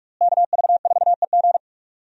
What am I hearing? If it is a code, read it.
Morse code, heard as a keyed tone, about 42 wpm, 700 Hz. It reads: KV4EG
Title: USA Call Signs - 5 Characters - Repeated 3x - 45wpm